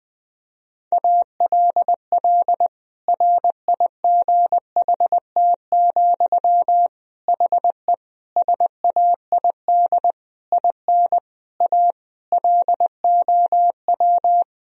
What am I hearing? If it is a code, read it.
Morse code, heard as a keyed tone, about 21 wpm, 700 Hz. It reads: ALL RIGHT, HE SAID IN A LOW
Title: Sentences from Top 1000 Words - Review Part 1 - 20wpm